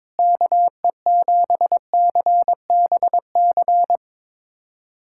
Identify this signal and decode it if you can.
Morse code, heard as a keyed tone, about 22 wpm, 700 Hz. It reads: KE7CBC
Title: USA Callsign DE Callsign 22wpm